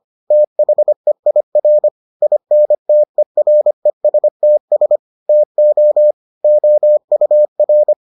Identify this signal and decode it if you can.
Morse code, heard as a keyed tone, about 25 wpm, 600 Hz. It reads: THEIR INTERESTS TO OUR